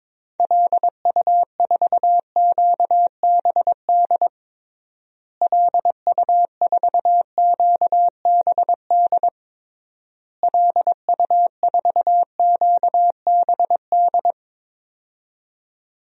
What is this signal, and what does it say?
Morse code, heard as a keyed tone, about 22 wpm, 700 Hz. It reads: LU4QBD LU4QBD LU4QBD
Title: DXCC-100 Call Signs - 6 Characters - Repeated 3x - 22wpm